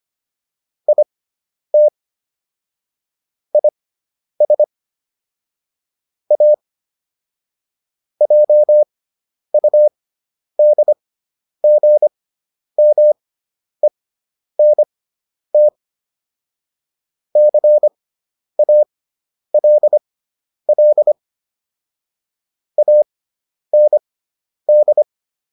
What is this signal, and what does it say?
Morse code, heard as a keyed tone, about 25 wpm, spaced as for 10 wpm, 600 Hz. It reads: IT IS A JUDGMENT CALL AND